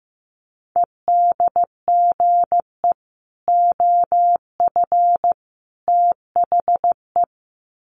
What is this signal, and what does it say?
Morse code, heard as a keyed tone, about 15 wpm, 700 Hz. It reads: EDGE OF THE
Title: Sets of 3 Words 15wpm